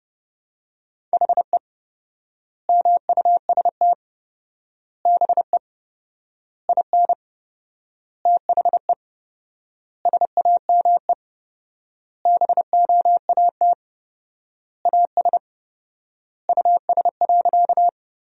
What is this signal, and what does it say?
Morse code, heard as a keyed tone, about 32 wpm, 700 Hz. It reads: HE MUST BE IN THE SAME BOAT AS US.
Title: 4x Word-Spacing - Sentences from Top 500 Words - 30wpm